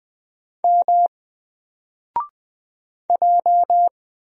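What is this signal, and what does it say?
Morse code, heard as a keyed tone, about 20 wpm, 700 Hz. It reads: M J
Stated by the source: Single letter and number Morse drill at 20 wpm, from N4LCD.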